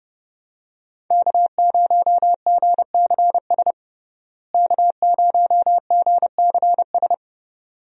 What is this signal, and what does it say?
Morse code, heard as a keyed tone, about 31 wpm, 700 Hz. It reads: K0GCS K0GCS
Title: USA Call Signs - 5 Characters - Repeated 2x - 30wpm